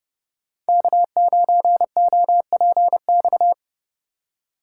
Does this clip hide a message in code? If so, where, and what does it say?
Yes: Morse code from 0.7 to 3.5 s, heard as a keyed tone, about 40 wpm, 700 Hz. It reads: K9OPX